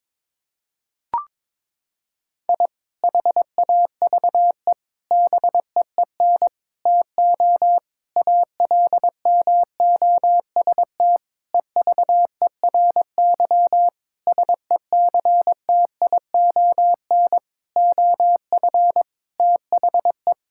Morse code, heard as a keyed tone, about 22 wpm, 700 Hz. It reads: I HAVE BEEN TO ALMOST EVERY SECTION OF THE
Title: Sentences from Top 1000 Words - Review Part 5 - 22wpm